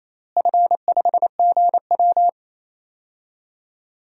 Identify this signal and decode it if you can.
Morse code, heard as a keyed tone, about 28 wpm, 700 Hz. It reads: F5GW
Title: DXCC-100 Call Signs - 4 Characters - Repeated 3x - 28wpm